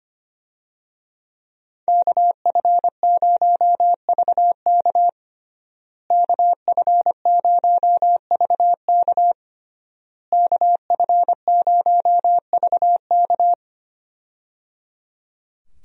Morse code, heard as a keyed tone, about 25 wpm, 700 Hz. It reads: KF0VK KF0VK KF0VK
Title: USA Call Signs - 5 Characters - Repeated 3x - 25wpm